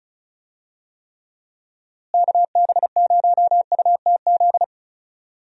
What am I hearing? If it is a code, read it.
Morse code, heard as a keyed tone, about 34 wpm, 700 Hz. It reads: KB0UTZ
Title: USA Call Signs - 6 Characters - Repeated 3x - 35wpm